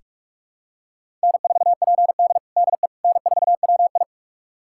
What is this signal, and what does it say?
Morse code, heard as a keyed tone, about 47 wpm, 700 Hz. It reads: N4PD DE N4WI
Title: USA Callsign DE Callsign - 4 Characters - 45wpm